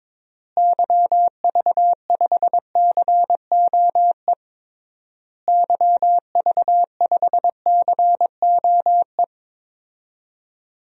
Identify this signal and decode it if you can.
Morse code, heard as a keyed tone, about 22 wpm, 700 Hz. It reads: YV5COE YV5COE
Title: DXCC-100 Call Signs - 6 Characters - Repeated 3x - 22wpm